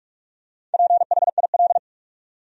Morse code, heard as a keyed tone, about 46 wpm, 700 Hz. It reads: PHIL
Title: USA Common Names - 45wpm